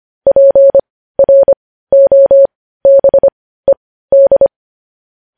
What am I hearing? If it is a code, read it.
Morse code, heard as a keyed tone, about 25 wpm, spaced as for 15 wpm, 550 Hz. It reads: PROBED
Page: news headlines in morse code at 15 wpm